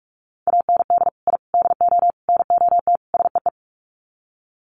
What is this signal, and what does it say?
Morse code, heard as a keyed tone, about 44 wpm, 700 Hz. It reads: AND I DO NOT SEE
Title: Sets of 5 Words Encore2 45wpm